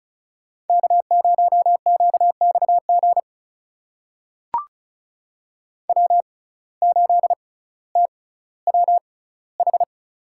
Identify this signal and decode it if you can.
Morse code, heard as a keyed tone, about 36 wpm, spaced as for 18 wpm, 700 Hz. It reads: K0QXG W8TWH
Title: USA Call Signs - 5 Characters - 6x Character Spacing - 35wpm